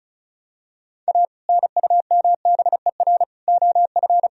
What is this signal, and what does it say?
Morse code, heard as a keyed tone, about 36 wpm, 700 Hz. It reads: A NUMBER OF
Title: Sentences from Top 1000 Words - Review Part 1 - 35wpm